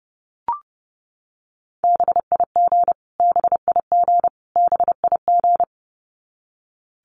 Morse code, heard as a keyed tone, about 30 wpm, 700 Hz. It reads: BIG BIG BIG
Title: Top 200 Words Repeated 30wpm